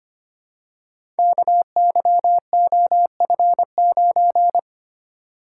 Morse code, heard as a keyed tone, about 25 wpm, 700 Hz. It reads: KYOF9
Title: Code Groups - 5 Letter:Numbers - 25wpm